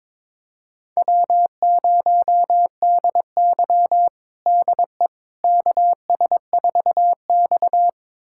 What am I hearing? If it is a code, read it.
Morse code, heard as a keyed tone, about 22 wpm, 700 Hz. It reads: W0DY DE KS4X
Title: USA Callsign DE Callsign - 4 Characters - 22wpm